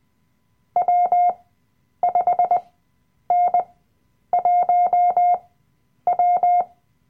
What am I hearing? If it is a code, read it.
Morse code, heard as a keyed tone, about 20 wpm, spaced as for 9 wpm, 710 Hz. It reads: W5N1W
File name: Workshop Dateien / CW-Online-Kurs / Lektion 007